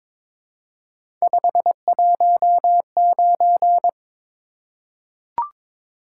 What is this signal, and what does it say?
Morse code, heard as a keyed tone, about 22 wpm, 700 Hz. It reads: 519
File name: practice / 22wpm / Three Letter-Number Combinations - 22wpm